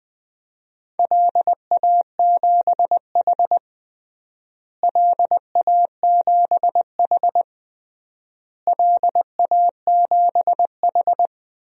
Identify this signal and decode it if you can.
Morse code, heard as a keyed tone, about 21 wpm, 700 Hz. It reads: LA7H LA7H LA7H
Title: DXCC-100 Call Signs - 4 Characters - Repeated 3x - 20wpm